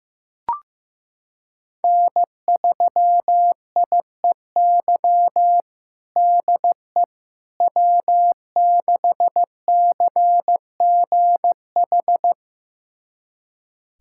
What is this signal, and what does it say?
Morse code, heard as a keyed tone, about 16 wpm, 700 Hz. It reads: N3IEY DE W6CGH
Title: USA Callsign DE Callsign - 5 Characters - 15wpm